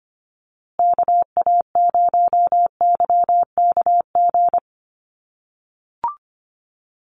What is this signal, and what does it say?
Morse code, heard as a keyed tone, about 25 wpm, 700 Hz. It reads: KA0YKG
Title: USA Call Signs Repeated 3x 25wpm